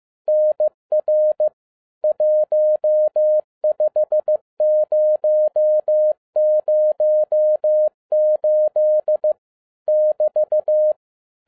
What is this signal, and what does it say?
Morse code, heard as a keyed tone, about 15 wpm, 610 Hz. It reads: NR 15008 <BT>
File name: 2012-01-mp3 / n17-SUMMARY OF FEA NET IN 2011-15wpm